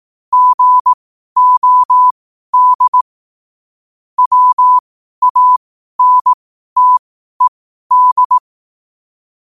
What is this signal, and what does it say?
Morse code, heard as a keyed tone, about 18 wpm, spaced as for 12 wpm, 1000 Hz. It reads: GOD WANTED
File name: qotdfiles / 12wpm